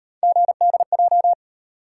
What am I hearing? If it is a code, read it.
Morse code, heard as a keyed tone, about 37 wpm, 700 Hz. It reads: GDJ